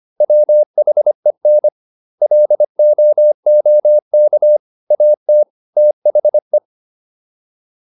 Morse code, heard as a keyed tone, about 25 wpm, 600 Hz. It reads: WHEN LOOK AT THE